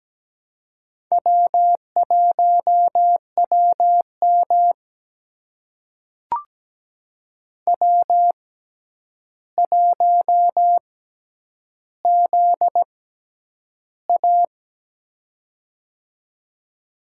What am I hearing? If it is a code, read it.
Morse code, heard as a keyed tone, about 17 wpm, spaced as for 9 wpm, 700 Hz. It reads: W1WM W1ZA
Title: USA Call Signs - 4 Characters - 6x Character Spacing - 17wpm